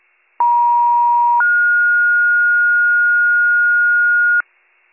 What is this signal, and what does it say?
Two-tone page, heard as a one-second tone, then a three-second tone, about 950 then 1430 Hz, with noise. It is heard from 0.4 to 4.4 s.